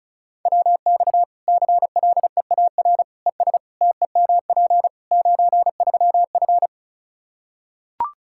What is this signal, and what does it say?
Morse code, heard as a keyed tone, about 36 wpm, 700 Hz. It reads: WX CLEAR ES TEMP 93F
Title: K1USN SST QSO Practice 35wpm